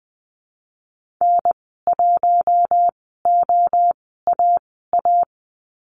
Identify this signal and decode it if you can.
Morse code, heard as a keyed tone, about 20 wpm, spaced as for 15 wpm, 700 Hz. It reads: N1OAA
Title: 048 Lesson 9g - Number 4 - Call Signs 20wpm